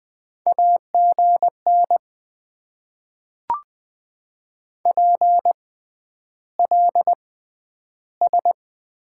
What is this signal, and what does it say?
Morse code, heard as a keyed tone, about 20 wpm, spaced as for 10 wpm, 700 Hz. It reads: AGN PLS
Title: Top 90 QSO Elements - 6x Character Spacing - 20wpm